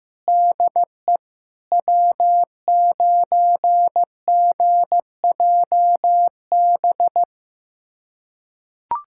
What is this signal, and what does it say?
Morse code, heard as a keyed tone, about 15 wpm, 700 Hz. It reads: DE W9GJB
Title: USA Callsign DE Callsign - 5 Characters - 15wpm